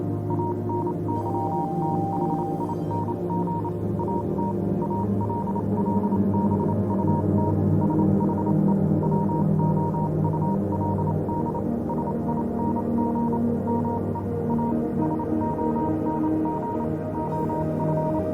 Morse code, heard as a keyed tone, about 28 wpm, spaced as for 25 wpm, 940 Hz, with noise. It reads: ANYTHING AT ALL BUT SHE WOULD SING ME A SONG OF
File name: Vylet Pony / aura / do you remember the song she sang (morse code version)